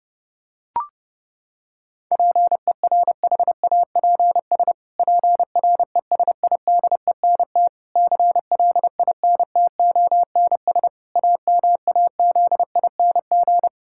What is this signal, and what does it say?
Morse code, heard as a keyed tone, about 32 wpm, 700 Hz. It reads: PERHAPS PRESIDENT CLINTONS AMAZING
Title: Pangram Challenge v2 - 30wpm